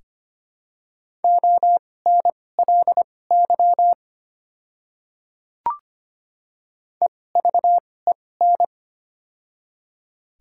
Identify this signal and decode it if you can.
Morse code, heard as a keyed tone, about 25 wpm, spaced as for 18 wpm, 700 Hz. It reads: ONLY EVEN
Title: 2x Character Spacing - Top 200 Words - 25wpm